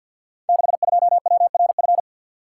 Morse code, heard as a keyed tone, about 52 wpm, 700 Hz. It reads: 62WRF
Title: Code Groups - 5 Letter:Numbers - 50wpm